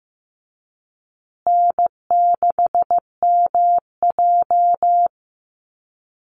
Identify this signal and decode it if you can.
Morse code, heard as a keyed tone, about 15 wpm, 700 Hz. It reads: N6MJ
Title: Easy Call Signs Repeated 3x - 15wpm